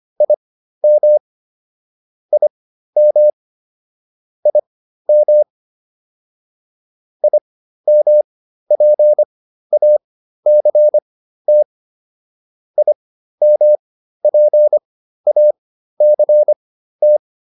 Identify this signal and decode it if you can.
Morse code, heard as a keyed tone, about 25 wpm, spaced as for 13 wpm, 600 Hz. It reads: IM IM IM IMPACT IMPACT